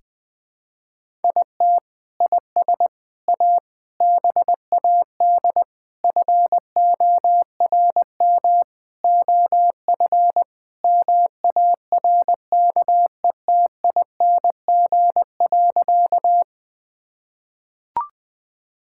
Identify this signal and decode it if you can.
Morse code, heard as a keyed tone, about 20 wpm, 700 Hz. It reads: IT IS A BAD FORM OF MARKETING.
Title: Sentences from Top 1000 Words - Review Part 5 - 20wpm